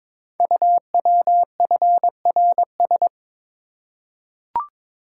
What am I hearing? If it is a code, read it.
Morse code, heard as a keyed tone, about 22 wpm, 700 Hz. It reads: UWFRS
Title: Code Groups - 5 Letters - 22wpm